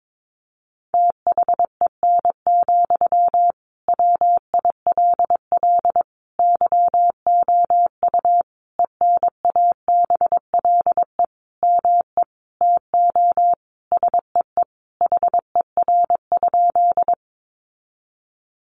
Morse code, heard as a keyed tone, about 22 wpm, 700 Hz. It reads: THEN, WILL YOU ENABLE ME TO SEE HER?
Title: Sentences from Top 1500 Words 22wpm